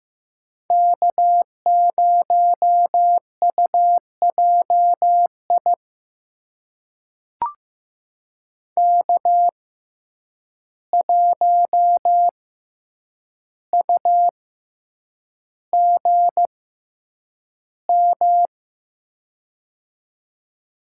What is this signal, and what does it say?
Morse code, heard as a keyed tone, about 15 wpm, spaced as for 8 wpm, 700 Hz. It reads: K0UJI K1UGM
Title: USA Call Signs - 5 Characters - 6x Character Spacing - 15wpm